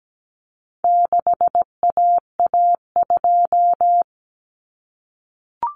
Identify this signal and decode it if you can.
Morse code, heard as a keyed tone, about 17 wpm, 700 Hz. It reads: 6AA2
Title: Common Cut Numbers - 4-Digit 17wpm